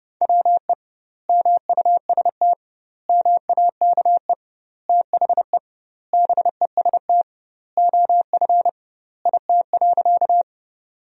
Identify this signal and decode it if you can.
Morse code, heard as a keyed tone, about 30 wpm, 700 Hz. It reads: WE MUST MAKE THE BEST OF IT.
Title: 2x Word-Spacing - Sentences from Top 400 Words - 30wpm